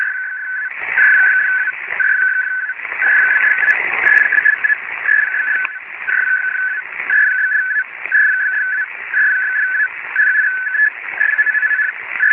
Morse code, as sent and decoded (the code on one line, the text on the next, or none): none